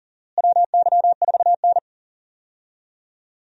Morse code, heard as a keyed tone, about 40 wpm, 700 Hz. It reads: WY4N
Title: USA Call Signs - 4 Characters - Repeated 2x - 40wpm